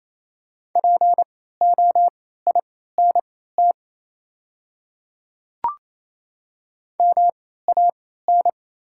Morse code, heard as a keyed tone, about 28 wpm, spaced as for 16 wpm, 700 Hz. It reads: POINT MAN
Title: 3x Character Spacing - Top 200 Words - 28wpm